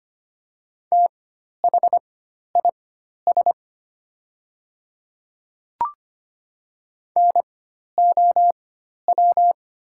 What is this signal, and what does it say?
Morse code, heard as a keyed tone, about 25 wpm, spaced as for 12 wpm, 700 Hz. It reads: THIS NOW
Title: 4x Character Spacing - Top 200 Words - 25wpm